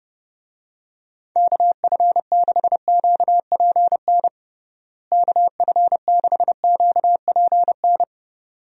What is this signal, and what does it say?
Morse code, heard as a keyed tone, about 32 wpm, 700 Hz. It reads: KF6QPN KF6QPN
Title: USA Call Signs - 6 Characters - Repeated 2x - 30wpm